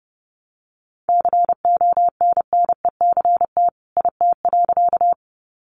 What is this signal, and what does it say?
Morse code, heard as a keyed tone, about 30 wpm, 700 Hz. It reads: CONNECT IT.
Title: Sentences from Top 1500 Words 30wpm